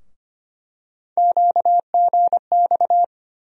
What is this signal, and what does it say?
Morse code, heard as a keyed tone, about 25 wpm, 700 Hz. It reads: QGX
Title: USA Call Sign Postfixes - 3 Letters 25wpm